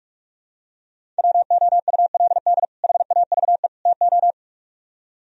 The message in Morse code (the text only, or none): WOULD HAVE TO